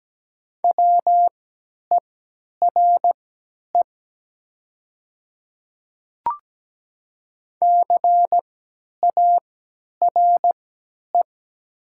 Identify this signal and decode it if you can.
Morse code, heard as a keyed tone, about 17 wpm, spaced as for 10 wpm, 700 Hz. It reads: WERE CARE
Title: 3x Character Spacing - Top 300 Words - 17wpm